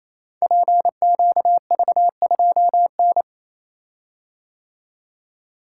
Morse code, heard as a keyed tone, about 28 wpm, 700 Hz. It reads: PQV2N
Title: Code Groups - 5 Letter:Numbers - 28wpm